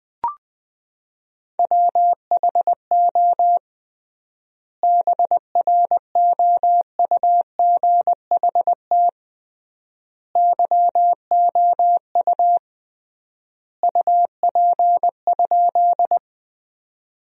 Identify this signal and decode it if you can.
Morse code, heard as a keyed tone, about 20 wpm, 700 Hz. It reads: WHO BROUGHT YOU UP?
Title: 3x Word-Spacing - Sentences from Top 500 Words - 20wpm